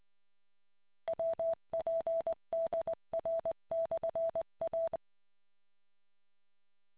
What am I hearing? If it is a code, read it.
Morse code, heard as a keyed tone, about 20 wpm, 660 Hz. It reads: WPDR/A